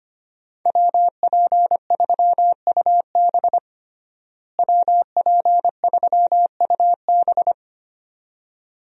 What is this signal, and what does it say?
Morse code, heard as a keyed tone, about 25 wpm, 700 Hz. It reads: WP3UB WP3UB
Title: USA Call Signs - 5 Characters - Repeated 2x - 25wpm